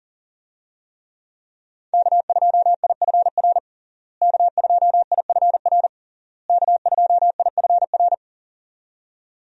Morse code, heard as a keyed tone, about 40 wpm, 700 Hz. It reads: K2IFR K2IFR K2IFR
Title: USA Call Signs - 5 Characters - Repeated 3x - 40wpm